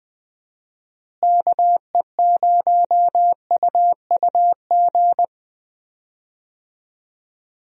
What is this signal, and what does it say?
Morse code, heard as a keyed tone, about 20 wpm, 700 Hz. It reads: KE0UUG
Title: USA Call Signs - 6 Characters - Repeated 2x - 20wpm